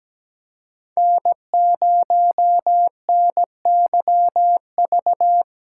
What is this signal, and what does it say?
Morse code, heard as a keyed tone, about 17 wpm, 700 Hz. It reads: N0NYV
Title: USA Callsign DE Callsign - 5 Characters - 17wpm